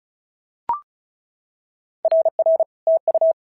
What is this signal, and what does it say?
Morse code, heard as a keyed tone, about 34 wpm, 650 Hz. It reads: RR TU
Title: SOTA QSO Practice 35wpm